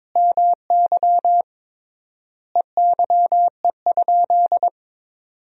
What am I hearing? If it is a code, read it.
Morse code, heard as a keyed tone, about 21 wpm, 700 Hz. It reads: MY EYE?
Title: 3x Word-Spacing - Sentences from Top 500 Words - 22wpm